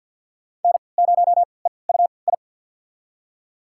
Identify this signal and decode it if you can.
Morse code, heard as a keyed tone, about 46 wpm, spaced as for 27 wpm, 700 Hz. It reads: N0EUI
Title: USA Call Signs - 5 Characters - 3x Character Spacing - 50wpm